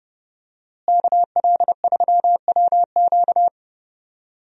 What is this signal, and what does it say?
Morse code, heard as a keyed tone, about 28 wpm, 700 Hz. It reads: KL3WQ
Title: USA Call Signs - 5 Characters - Repeated 3x - 30wpm